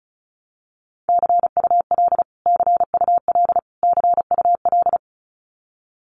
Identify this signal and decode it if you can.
Morse code, heard as a keyed tone, about 34 wpm, 700 Hz. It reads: CUL CUL CUL
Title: Abbreviations Repeated 3x - 35wpm